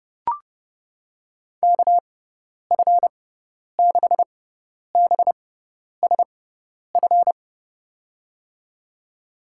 Morse code, heard as a keyed tone, about 30 wpm, spaced as for 10 wpm, 700 Hz. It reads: KF6BSF
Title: USA Call Signs - 6 Characters - 6x Character Spacing - 30wpm